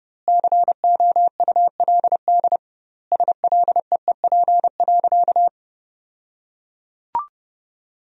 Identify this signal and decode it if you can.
Morse code, heard as a keyed tone, about 30 wpm, 700 Hz. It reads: COULD SLEEP.
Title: 2x Word-Spacing - Sentences from Top 500 Words - 30wpm